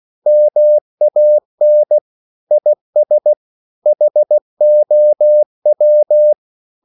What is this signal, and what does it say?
Morse code, heard as a keyed tone, about 16 wpm, 600 Hz. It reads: MAN IS HOW
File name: CW / Practice / EnglishQuotations0216